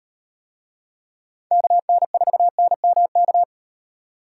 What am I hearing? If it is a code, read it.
Morse code, heard as a keyed tone, about 38 wpm, 700 Hz. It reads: KN4NMK